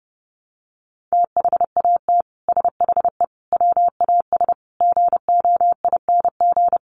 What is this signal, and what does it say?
Morse code, heard as a keyed tone, about 30 wpm, 700 Hz. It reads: THAT SHE WAS GOING
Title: Sets of 5 Words Encore2 30wpm